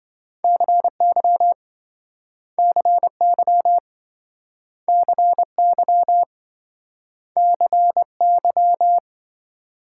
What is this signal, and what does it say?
Morse code, heard as a keyed tone, about 28 wpm, 700 Hz. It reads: CY CY CY CY